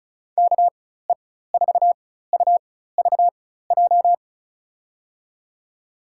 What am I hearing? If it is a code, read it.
Morse code, heard as a keyed tone, about 36 wpm, spaced as for 16 wpm, 700 Hz. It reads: KE4UVJ
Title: USA Call Signs - 6 Characters - 4x Character Spacing - 35wpm